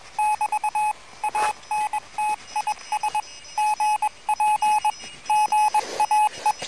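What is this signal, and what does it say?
Morse code, heard as a keyed tone, about 21 wpm, 850 Hz. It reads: <BT> ANTIS GP GAE
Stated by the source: QRP contact between UA1CEG and UA3YLP using homemade gear on 14060 kHz during a QRP round table, with heavy interference from a contest.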